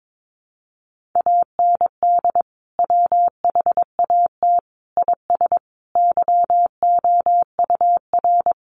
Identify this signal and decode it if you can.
Morse code, heard as a keyed tone, about 22 wpm, 700 Hz. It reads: AND WHAT IS YOUR
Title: Sentences from Top 1200 Words 22wpm